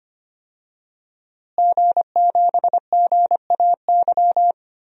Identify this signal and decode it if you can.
Morse code, heard as a keyed tone, about 25 wpm, 700 Hz. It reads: G7GAY
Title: DXCC-100 Call Signs - 5 Characters - Repeated 3x - 25wpm